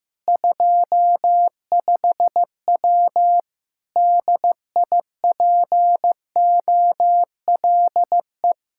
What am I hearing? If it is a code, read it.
Morse code, heard as a keyed tone, about 15 wpm, 700 Hz. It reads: 25W DIPOLE
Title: K1USN SST QSO Practice 15wpm